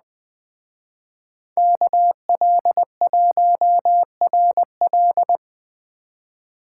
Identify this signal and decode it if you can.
Morse code, heard as a keyed tone, about 20 wpm, 700 Hz. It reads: KL1RL